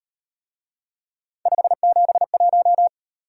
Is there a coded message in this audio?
Yes: Morse code from 1.5 to 2.9 s, heard as a keyed tone, about 37 wpm, 700 Hz. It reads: 571